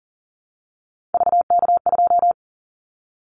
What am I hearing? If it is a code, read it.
Morse code, heard as a keyed tone, about 40 wpm, 700 Hz. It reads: VK2
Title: Callsign Prefixes for Top 100 DXCC - 40wpm